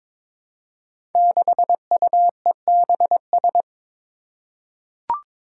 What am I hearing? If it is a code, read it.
Morse code, heard as a keyed tone, about 22 wpm, 700 Hz. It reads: 6UEBS
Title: Five Challenging Letter-Number Combinations 22wpm